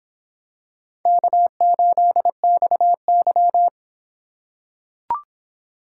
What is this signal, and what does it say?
Morse code, heard as a keyed tone, about 23 wpm, 700 Hz. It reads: K8XY